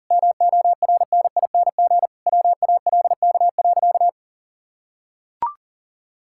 Morse code, heard as a keyed tone, about 37 wpm, 700 Hz. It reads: MORNING WALK.
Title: Sentences from Top 1000 Words - Review Part 5 - 40wpm